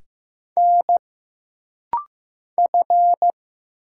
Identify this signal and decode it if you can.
Morse code, heard as a keyed tone, about 15 wpm, 700 Hz. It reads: N F
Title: Single Letters - Rapid-Fire 15wpm